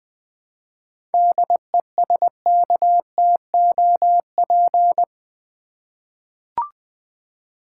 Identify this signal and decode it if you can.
Morse code, heard as a keyed tone, about 20 wpm, 700 Hz. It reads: DESKTOP
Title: 7 Letter Words - First 3 Letters Given - 20wpm